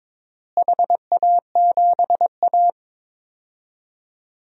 Morse code, heard as a keyed tone, about 21 wpm, 700 Hz. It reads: HA7A
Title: DXCC-100 Call Signs - 4 Characters - Repeated 2x - 22wpm